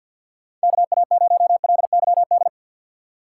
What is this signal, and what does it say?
Morse code, heard as a keyed tone, about 53 wpm, 700 Hz. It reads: KA0LYD
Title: USA Call Signs - 6x Character Spacing - 50wpm